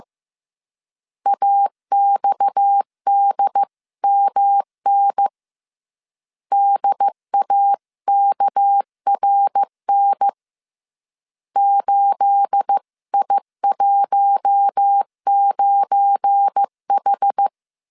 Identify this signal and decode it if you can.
Morse code, heard as a keyed tone, about 15 wpm, 800 Hz. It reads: AXDMN DAKRN 8I19H
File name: CW / DET / 1 9 1500C S C 14wpm 3245